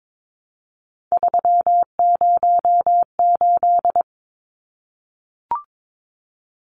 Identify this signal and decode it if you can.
Morse code, heard as a keyed tone, about 22 wpm, 700 Hz. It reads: 308
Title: Numbers - 3-Digit 22wpm